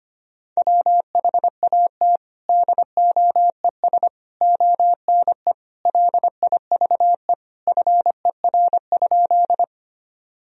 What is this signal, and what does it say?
Morse code, heard as a keyed tone, about 25 wpm, 700 Hz. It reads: WHAT DOES ONE LIVE FER?
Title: Sentences from Top 200 Words and Abbreviations 25wpm